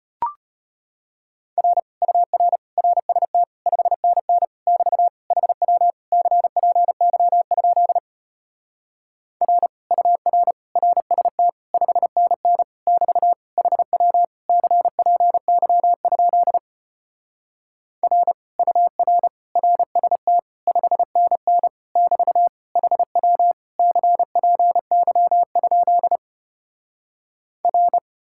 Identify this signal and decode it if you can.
Morse code, heard as a keyed tone, about 33 wpm, 700 Hz. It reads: R UR RST 5NN <BT> HW CPY? R UR RST 5NN <BT> HW CPY? R UR RST 5NN <BT> HW CPY? R